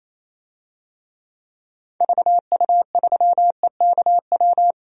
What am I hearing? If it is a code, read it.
Morse code, heard as a keyed tone, about 28 wpm, 700 Hz. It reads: VU3EKW
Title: DXCC-100 Call Signs - 6 Characters - Repeated 3x - 28wpm